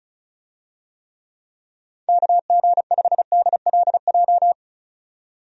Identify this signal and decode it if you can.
Morse code, heard as a keyed tone, about 36 wpm, 700 Hz. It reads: KG5DLJ
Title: USA Call Signs - 6 Characters - 35wpm